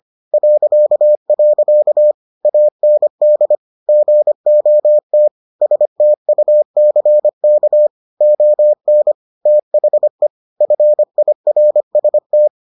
Morse code, heard as a keyed tone, about 25 wpm, 600 Hz. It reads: .. AND GOT STUCK ON THE FIRST